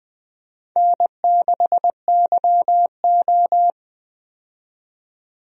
Morse code, heard as a keyed tone, about 20 wpm, 700 Hz. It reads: N6YO
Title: USA Call Signs - 4 Characters - 20wpm